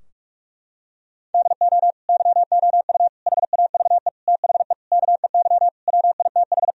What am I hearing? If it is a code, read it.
Morse code, heard as a keyed tone, about 41 wpm, 700 Hz. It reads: DO YOU HAVE THE KEY WITH
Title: Sentences from Top 1000 Words - Review Part 4 - 45wpm